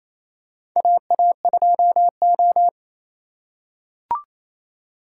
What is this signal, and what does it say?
Morse code, heard as a keyed tone, about 28 wpm, 700 Hz. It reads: AA2O
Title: USA Call Signs - 4 Characters - 4x Character Spacing - 28wpm